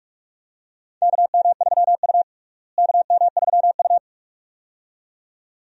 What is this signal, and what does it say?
Morse code, heard as a keyed tone, about 42 wpm, 700 Hz. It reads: KM3U KM3U
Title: USA Call Signs - 4 Characters - Repeated 2x - 45wpm